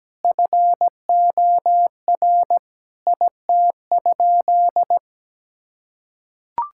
Morse code, heard as a keyed tone, about 17 wpm, 700 Hz. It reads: FOR IT?
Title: Sentences from Top 1000 Words - Review Part 6 - 17wpm